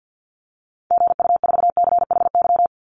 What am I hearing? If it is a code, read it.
Morse code, heard as a keyed tone, about 49 wpm, 700 Hz. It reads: GU4CHY
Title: Call Signs - 50wpm